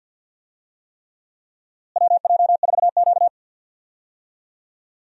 Morse code, heard as a keyed tone, about 51 wpm, 700 Hz. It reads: WJ4Q
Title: USA Call Signs - 4 Characters - Repeated 3x - 50wpm